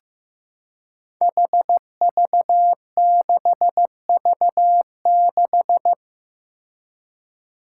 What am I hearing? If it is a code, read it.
Morse code, heard as a keyed tone, about 15 wpm, 700 Hz. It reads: HV6V6
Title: Five Challenging Letter-Number Combinations 15wpm